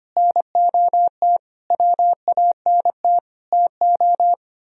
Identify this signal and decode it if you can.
Morse code, heard as a keyed tone, about 33 wpm, 700 Hz. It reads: NOT WANT TO